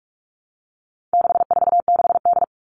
Morse code, heard as a keyed tone, about 44 wpm, 700 Hz. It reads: 646D
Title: All Cut Numbers - 4-Digit 45wpm